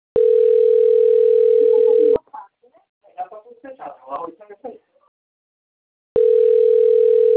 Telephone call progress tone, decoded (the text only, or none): ringback tone